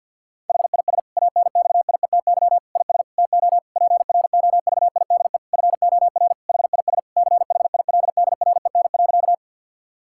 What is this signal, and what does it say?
Morse code, heard as a keyed tone, about 50 wpm, 700 Hz. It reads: HIS ANXIETY IS TO PROVIDE FOR HIS CHILDREN.
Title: Sentences from Top 1600 Words 50wpm